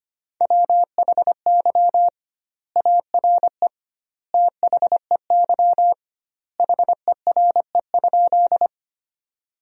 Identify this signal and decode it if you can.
Morse code, heard as a keyed tone, about 25 wpm, 700 Hz. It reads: WHY ARE THEY HERE?
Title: 2x Word-Spacing - Sentences from Top 200 Words - 25wpm